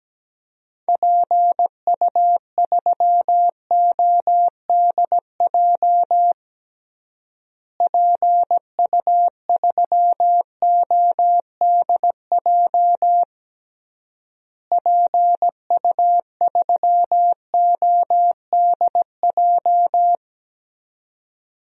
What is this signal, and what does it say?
Morse code, heard as a keyed tone, about 17 wpm, 700 Hz. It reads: PU3ODJ PU3ODJ PU3ODJ
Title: DXCC-100 Call Signs - 6 Characters - Repeated 3x - 17wpm